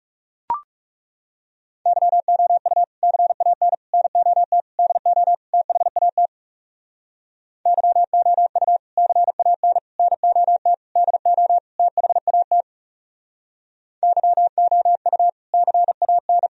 Morse code, heard as a keyed tone, about 37 wpm, 700 Hz. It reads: YOU CAN NOT DO THAT YOU CAN NOT DO THAT YOU CAN
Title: Speed-Racing - Sets of 5 Words 30wpm